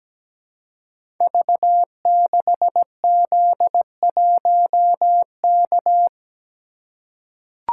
Morse code, heard as a keyed tone, about 17 wpm, 700 Hz. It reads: V6Z1K
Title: Code Groups - 5 Letter:Numbers - 17wpm